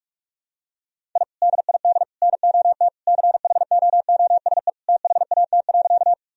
Morse code, heard as a keyed tone, about 41 wpm, 700 Hz. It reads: I DID NOT CHOOSE THAT.
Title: Sentences from Top 1000 Words - Review Part 4 - 45wpm